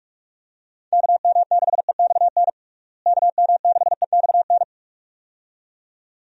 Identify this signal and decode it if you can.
Morse code, heard as a keyed tone, about 46 wpm, 700 Hz. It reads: KM6EXN KM6EXN
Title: USA Call Signs - 6 Characters - Repeated 2x - 45wpm